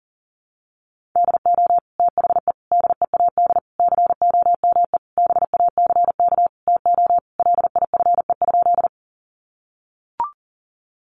Morse code, heard as a keyed tone, about 40 wpm, 700 Hz. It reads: DO THE DEAD COME BACK TO LIFE?
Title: Sentences from Top 1500 Words 40wpm